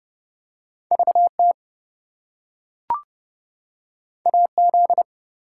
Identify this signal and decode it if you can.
Morse code, heard as a keyed tone, about 30 wpm, 700 Hz. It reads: VT AZ
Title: US States and Canadian Provinces 30wpm